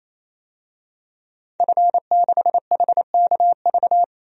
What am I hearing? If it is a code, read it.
Morse code, heard as a keyed tone, about 28 wpm, 700 Hz. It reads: F6HKV
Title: DXCC-100 Call Signs - 5 Characters - 28wpm